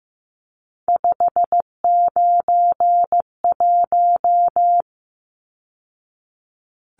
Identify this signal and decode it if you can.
Morse code, heard as a keyed tone, about 15 wpm, 700 Hz. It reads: 591
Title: Numbers - 3-Digit 15wpm